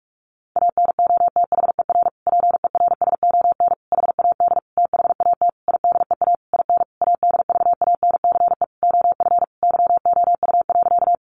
Morse code, heard as a keyed tone, about 44 wpm, 700 Hz. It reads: ANOTHER PERSON HAD THAT IDEA IN ADVANCE OF YOU.
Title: Sentences from Top 1400 Words 45wpm